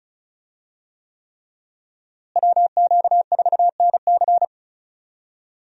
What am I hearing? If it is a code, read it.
Morse code, heard as a keyed tone, about 34 wpm, 700 Hz. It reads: WQ4NC
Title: USA Call Signs - 5 Characters - Repeated 3x - 35wpm